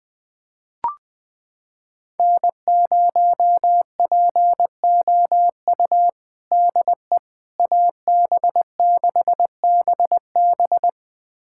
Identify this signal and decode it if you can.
Morse code, heard as a keyed tone, about 20 wpm, 700 Hz. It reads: N0POU DE AB6BB
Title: USA Callsign DE Callsign - 5 Characters - 20wpm